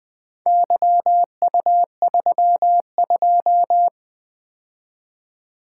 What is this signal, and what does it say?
Morse code, heard as a keyed tone, about 20 wpm, 700 Hz. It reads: YU32
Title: Four Letter-Number Combinations - 20wpm